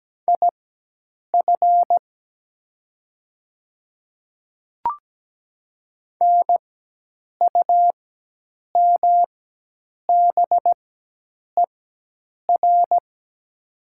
Morse code, heard as a keyed tone, about 17 wpm, spaced as for 8 wpm, 700 Hz. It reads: IF NUMBER
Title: 4x Character Spacing - Top 200 Words - 17wpm